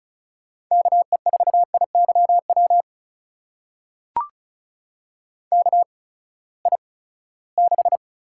Morse code, heard as a keyed tone, about 36 wpm, spaced as for 20 wpm, 700 Hz. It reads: KE4IYW KI6
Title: USA Call Signs - 6 Characters - 8x Character Spacing - 35wpm